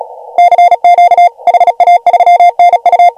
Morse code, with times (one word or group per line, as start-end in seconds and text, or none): 0.4-1.3 CQ
1.5-3.1 HA3NU